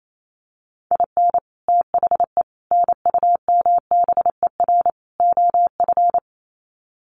Morse code, heard as a keyed tone, about 28 wpm, 700 Hz. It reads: IN THE NUMBER OF
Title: Sets of 4 Words 28wpm